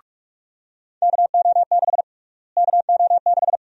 Morse code, heard as a keyed tone, about 42 wpm, 700 Hz. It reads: KO6 KO6
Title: USA Call Sign Prefixes - Repeated 2x - 45wpm